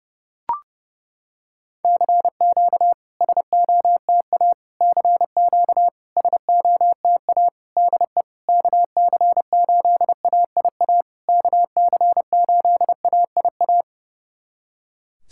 Morse code, heard as a keyed tone, about 30 wpm, 700 Hz. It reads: CQ SOTA CQ SOTA DE KC8AIA KC8AIA
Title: SOTA QSO Practice 30wpm